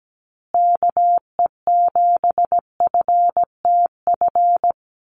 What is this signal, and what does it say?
Morse code, heard as a keyed tone, about 17 wpm, 700 Hz. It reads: KE7FTF
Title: USA Call Signs Repeated 2x 17wpm